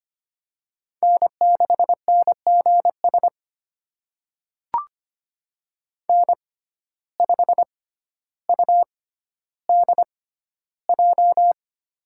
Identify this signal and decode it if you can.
Morse code, heard as a keyed tone, about 25 wpm, spaced as for 13 wpm, 700 Hz. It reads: N6NGS N5UDJ
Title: USA Call Signs - 6x Character Spacing - 25wpm